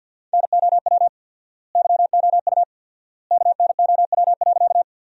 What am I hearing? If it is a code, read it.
Morse code, heard as a keyed tone, about 48 wpm, 700 Hz. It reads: NOW YOU KNOW.
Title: 4x Word-Spacing - Sentences from Top 100 Words - 50wpm